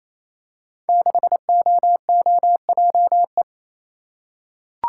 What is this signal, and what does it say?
Morse code, heard as a keyed tone, about 29 wpm, 700 Hz. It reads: 6OOJE
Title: Code Groups - 5 Letter:Numbers - 28wpm